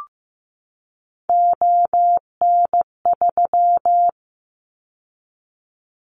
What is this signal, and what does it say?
Morse code, heard as a keyed tone, about 15 wpm, 700 Hz. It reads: ON3
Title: Callsign Prefixes for Top 100 DXCC - 15wpm